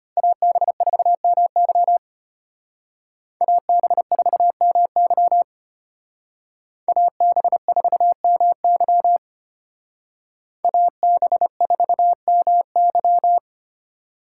Morse code, heard as a keyed tone, about 31 wpm, 700 Hz. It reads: AB4MY AB4MY AB4MY AB4MY